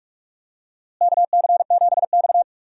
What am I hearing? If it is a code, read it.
Morse code, heard as a keyed tone, about 46 wpm, 700 Hz. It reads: KC7X
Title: USA Call Signs - 4 Characters - Repeated 3x - 45wpm